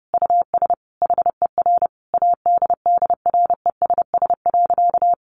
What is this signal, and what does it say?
Morse code, heard as a keyed tone, about 30 wpm, 700 Hz. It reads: US HER ADDRESS.
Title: Sentences from Top 1100 Words 30wpm